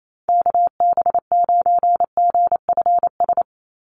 Morse code, heard as a keyed tone, about 28 wpm, 700 Hz. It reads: KB9GFS
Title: USA Call Signs 28wpm